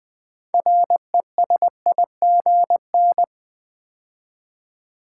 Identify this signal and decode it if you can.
Morse code, heard as a keyed tone, about 20 wpm, 700 Hz. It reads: RESIGN
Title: Build A Word - 2 Letters At A Time - 6-Letter Words - 20wpm